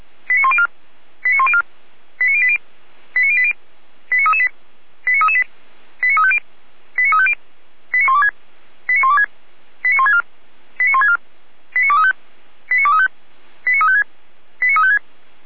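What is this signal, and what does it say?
Five-tone selective call (ZVEI), heard as quick runs of five tones, tones of about 70 ms. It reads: 89104 89104 89080 89080 89338 89338 89350 89350 89126 89126 89164 89164 89235 89235 89356 89356
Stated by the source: ZVEI five-tone call received on an SDR.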